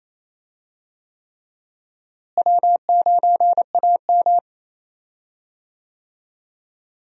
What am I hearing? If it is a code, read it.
Morse code, heard as a keyed tone, about 28 wpm, 700 Hz. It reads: W9AM
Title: USA Call Signs - 4 Characters - Repeated 3x - 28wpm